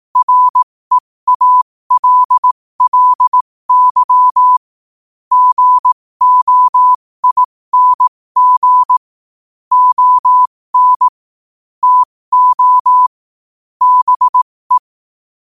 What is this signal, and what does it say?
Morse code, heard as a keyed tone, about 18 wpm, spaced as for 15 wpm, 1000 Hz. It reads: REALLY GOING ON TO BE
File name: qotdfiles / 15wpm